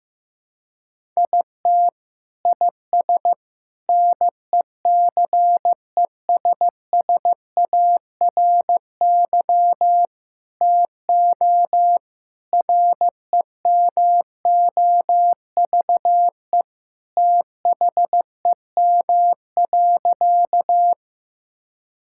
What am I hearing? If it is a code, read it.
Morse code, heard as a keyed tone, about 16 wpm, 700 Hz. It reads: IT IS NECESSARY TO REMOVE THEM.
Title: Sentences from Top 1000 Words - Review Part 5 - 15wpm